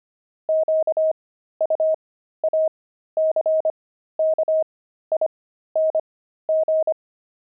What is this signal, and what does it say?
Morse code, heard as a keyed tone, about 25 wpm, spaced as for 13 wpm, 630 Hz. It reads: QUACKING
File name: pangram-fs-13-cs-25-five